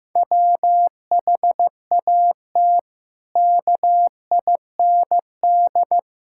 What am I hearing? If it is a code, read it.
Morse code, heard as a keyed tone, about 15 wpm, 700 Hz. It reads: WHAT KIND
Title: Sentences from Top 1000 Words - Review Part 5 - 15wpm